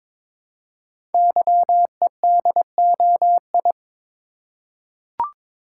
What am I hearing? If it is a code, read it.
Morse code, heard as a keyed tone, about 22 wpm, 700 Hz. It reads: YEDOI
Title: Code Groups - 5 Letters - 22wpm